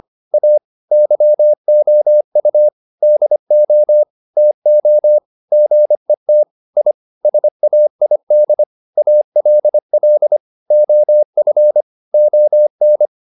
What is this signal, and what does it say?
Morse code, heard as a keyed tone, about 25 wpm, 600 Hz. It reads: A YOU DO TO GET I SAID ALL OF ON